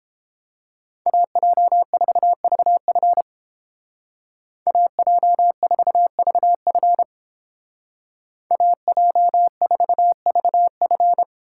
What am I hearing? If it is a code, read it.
Morse code, heard as a keyed tone, about 30 wpm, 700 Hz. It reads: AJ4VF AJ4VF AJ4VF